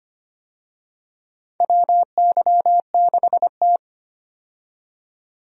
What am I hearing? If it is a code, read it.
Morse code, heard as a keyed tone, about 25 wpm, 700 Hz. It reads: WY6T
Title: USA Call Signs - 4 Characters - 25wpm